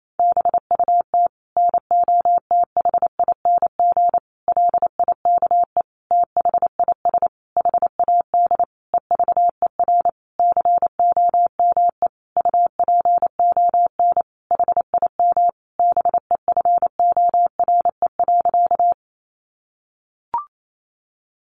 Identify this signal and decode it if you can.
Morse code, heard as a keyed tone, about 28 wpm, 700 Hz. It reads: BUT NOTHING LIKE THIS HAD EVER COME UPON HIM BEFORE.
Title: Sentences from Top 1100 Words 28wpm